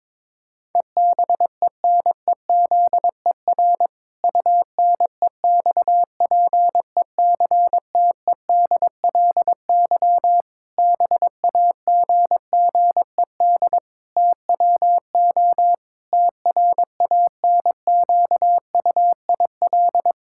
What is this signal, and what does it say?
Morse code, heard as a keyed tone, about 22 wpm, 700 Hz. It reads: EBENEZER UNEXPECTEDLY BAGGED TWO TRANQUIL
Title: Pangram Challenge v2 - 22wpm